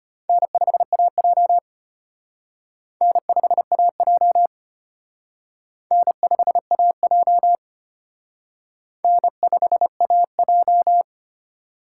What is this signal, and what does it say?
Morse code, heard as a keyed tone, about 32 wpm, 700 Hz. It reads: N5AJ N5AJ N5AJ N5AJ